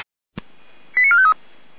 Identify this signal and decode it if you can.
Five-tone selective call (ZVEI), heard as quick runs of five tones, tones of about 70 ms. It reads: 89453